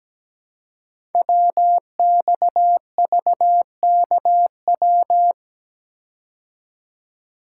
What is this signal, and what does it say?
Morse code, heard as a keyed tone, about 17 wpm, 700 Hz. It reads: WXVKW
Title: Code Groups - 5 Letters - 17wpm